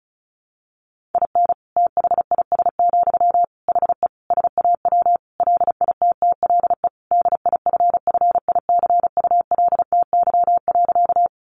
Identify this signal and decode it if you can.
Morse code, heard as a keyed tone, about 34 wpm, 700 Hz. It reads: IN THIS, HE SAW LITTLE DIFFICULTY.
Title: Sentences from Top 1500 Words 35wpm